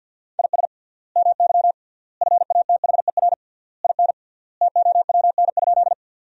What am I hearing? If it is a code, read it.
Morse code, heard as a keyed tone, about 52 wpm, 700 Hz. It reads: IS MY FATHER IN TOWN?
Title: 3x Word-Spacing - Sentences from Top 500 Words - 50wpm